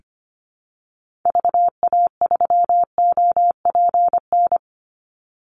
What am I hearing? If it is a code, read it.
Morse code, heard as a keyed tone, about 25 wpm, 700 Hz. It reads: VA3OPN
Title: Call Signs Repeated 3x - 25wpm